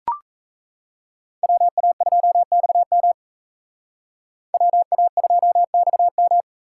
Morse code, heard as a keyed tone, about 38 wpm, 700 Hz. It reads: WA2XM WA2XM